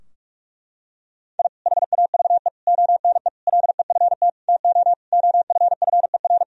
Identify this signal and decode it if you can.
Morse code, heard as a keyed tone, about 46 wpm, 700 Hz. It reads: I HAVE ONE LEFT TO OFFER
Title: Sentences from Top 1000 Words - Review Part 4 - 45wpm